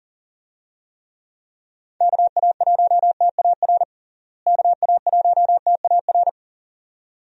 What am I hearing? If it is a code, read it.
Morse code, heard as a keyed tone, about 40 wpm, 700 Hz. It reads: KA1TAR KA1TAR
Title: USA Call Signs - 6 Characters - Repeated 2x - 40wpm